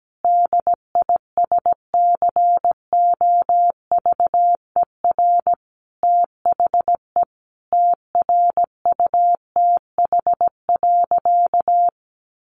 Morse code, heard as a keyed tone, about 17 wpm, 700 Hz. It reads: DISCOVER THE TRUTH.
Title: Sentences from Top 1100 Words 17wpm